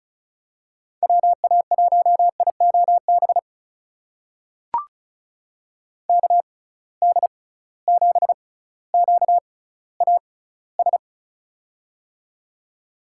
Morse code, heard as a keyed tone, about 36 wpm, spaced as for 18 wpm, 700 Hz. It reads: WA1IOB KD7QAS
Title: USA Call Signs - 6 Characters - 6x Character Spacing - 35wpm